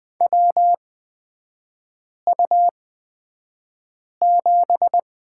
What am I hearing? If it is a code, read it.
Morse code, heard as a keyed tone, about 20 wpm, spaced as for 5 wpm, 700 Hz. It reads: WU7
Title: 243 Lesson 39e - Letter z - Three Characters 20wpm